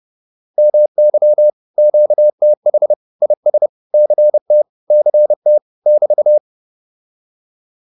Morse code, heard as a keyed tone, about 30 wpm, 600 Hz. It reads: MY QTH IS CT CT <BT>
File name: QSO206 30